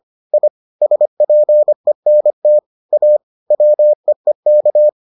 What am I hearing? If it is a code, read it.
Morse code, heard as a keyed tone, about 25 wpm, 600 Hz. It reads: I SPENT A WEEK